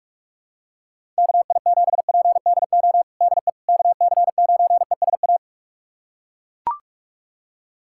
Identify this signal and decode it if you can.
Morse code, heard as a keyed tone, about 41 wpm, 700 Hz. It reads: KI7PDO DE KC9ESA
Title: USA Callsign DE Callsign - 6 Characters - 45wpm